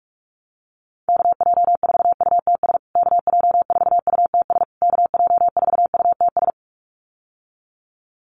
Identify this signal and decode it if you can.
Morse code, heard as a keyed tone, about 44 wpm, 700 Hz. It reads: KJ4UTS KJ4UTS KJ4UTS
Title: USA Call Signs Repeated 3x 45wpm